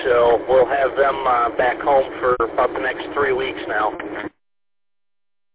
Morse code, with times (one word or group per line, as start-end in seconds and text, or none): none